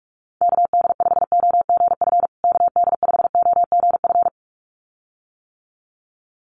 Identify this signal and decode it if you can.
Morse code, heard as a keyed tone, about 44 wpm, 700 Hz. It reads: KD5OGF KD5OGF
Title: USA Call Signs Repeated 2x 45wpm